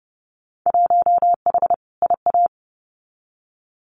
Morse code, heard as a keyed tone, about 30 wpm, 700 Hz. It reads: 1H IA
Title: Winter Field Day Exchanges 30wpm